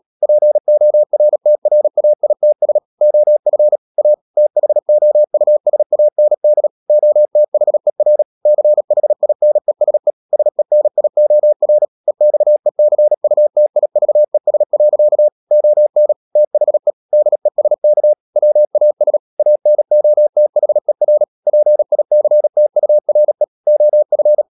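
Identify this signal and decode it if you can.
Morse code, heard as a keyed tone, about 37 wpm, 600 Hz. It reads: PORTRAITS OF A THOUSAND OTHER CHINESE SENIOR EXECUTIVES. ON THE DESK WAS ANOTHER PICTURE OF
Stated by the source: Morse code practice at 35–39 wpm sending English text of Ken Follett's novel Never, from CW med Gnister.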